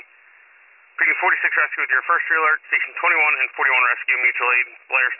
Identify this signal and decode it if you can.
Emergency Alert System (SAME) tones, heard as harsh buzzing bursts, with speech. It is heard from 0.9 to 4.4 s.